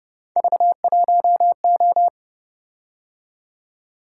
Morse code, heard as a keyed tone, about 28 wpm, 700 Hz. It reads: V1O